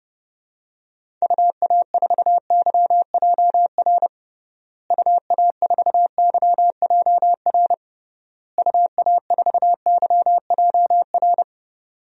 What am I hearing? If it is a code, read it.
Morse code, heard as a keyed tone, about 32 wpm, 700 Hz. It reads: UA4YJR UA4YJR UA4YJR
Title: DXCC-100 Call Signs - 6 Characters - Repeated 3x - 30wpm